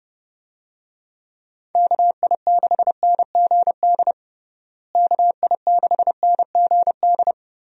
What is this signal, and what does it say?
Morse code, heard as a keyed tone, about 30 wpm, 700 Hz. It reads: KI6NGD KI6NGD
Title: USA Call Signs - 6 Characters - Repeated 3x - 30wpm